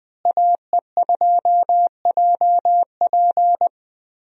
Morse code, heard as a keyed tone, about 20 wpm, 700 Hz. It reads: AE2JP
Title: USA Call Signs - 5 Characters - Repeated 3x - 20wpm